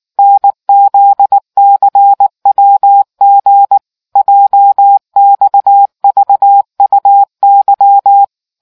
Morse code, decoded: NZCWG JXVUY

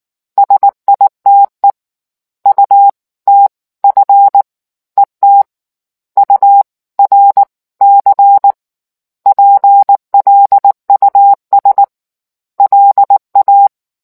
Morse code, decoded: SITE UTF ET U R C PLUS LA

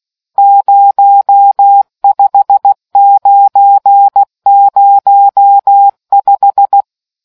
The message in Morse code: 05905

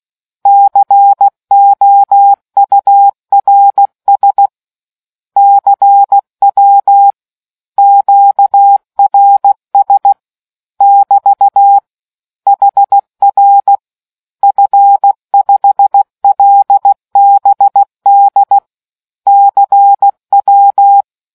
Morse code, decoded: COURS CW QRS <BT> HR F5LBD CW

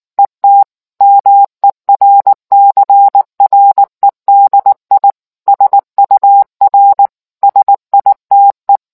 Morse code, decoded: ET MERCREDI SUR SITE

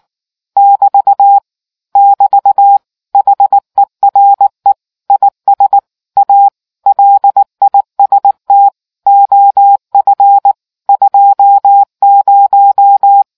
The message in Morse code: <BT> <BT> HERE IS A LIST OF 20

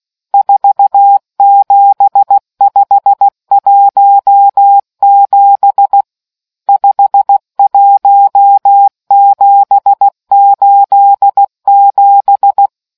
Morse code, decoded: 47517 51787

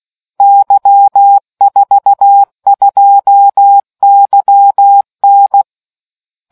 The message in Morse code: Y42YN

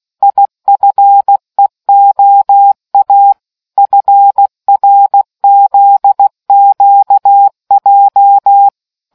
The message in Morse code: IFEOA FRZQJ

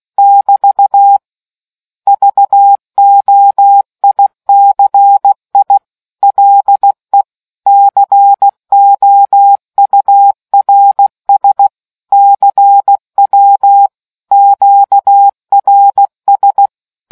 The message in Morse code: <BT> VOICI LE COURS CW QRS

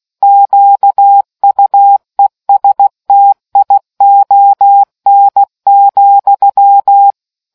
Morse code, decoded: QUESTION,

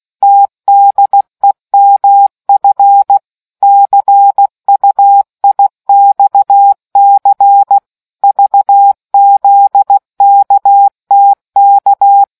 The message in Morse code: TDEMF CUIXC VZKTK

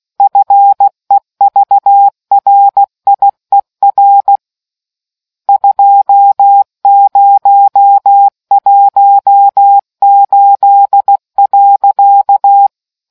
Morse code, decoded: FEVRIER 2018.

